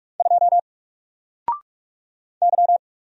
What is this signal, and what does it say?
Morse code, heard as a keyed tone, about 46 wpm, 700 Hz. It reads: 2 Y
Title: Single Letter-Number 45wpm